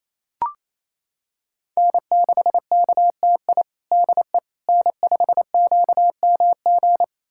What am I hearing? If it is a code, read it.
Morse code, heard as a keyed tone, about 28 wpm, 700 Hz. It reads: N6KTI DE N5QMG
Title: USA Callsign DE Callsign - 5 Characters - 28wpm